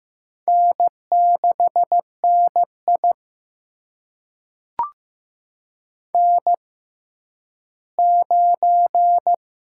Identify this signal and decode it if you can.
Morse code, heard as a keyed tone, about 15 wpm, spaced as for 10 wpm, 700 Hz. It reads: N6NI N9
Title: USA Call Signs - 6x Character Spacing - 15wpm